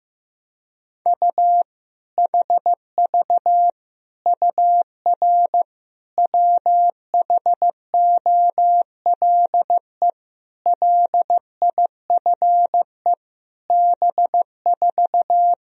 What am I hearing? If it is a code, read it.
Morse code, heard as a keyed tone, about 15 wpm, 700 Hz. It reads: U HV UR WHOLE LIFE B4
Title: Sentences from Top 400 Words and Abbreviations 15wpm